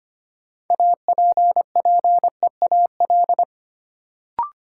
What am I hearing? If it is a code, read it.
Morse code, heard as a keyed tone, about 25 wpm, 700 Hz. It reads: APPEAL
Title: Sentences from Top 1300 Words - New Word Drill - 25wpm